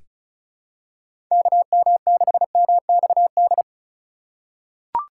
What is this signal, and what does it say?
Morse code, heard as a keyed tone, about 35 wpm, 700 Hz. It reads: KM6MXD
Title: USA Call Signs - 6 Characters - 3x Character Spacing - 35wpm